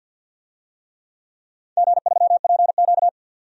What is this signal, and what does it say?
Morse code, heard as a keyed tone, about 47 wpm, 700 Hz. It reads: G3PQ
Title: DXCC-100 Call Signs - 4 Characters - 50wpm